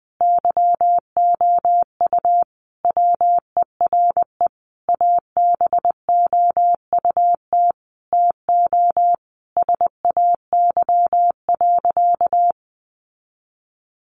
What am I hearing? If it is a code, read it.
Morse code, heard as a keyed tone, about 20 wpm, 700 Hz. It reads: YOU WERE ABOUT TO SAY.
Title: Sentences from Top 1200 Words 20wpm